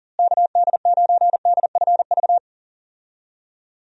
Morse code, heard as a keyed tone, about 40 wpm, 700 Hz. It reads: KD9DFV
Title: USA Call Signs - 6 Characters - Repeated 3x - 40wpm